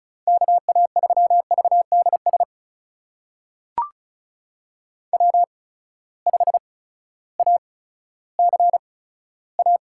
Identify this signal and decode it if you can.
Morse code, heard as a keyed tone, about 36 wpm, spaced as for 16 wpm, 700 Hz. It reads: KA3VDS W5ACA
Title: USA Call Signs - 8x Character Spacing - 35wpm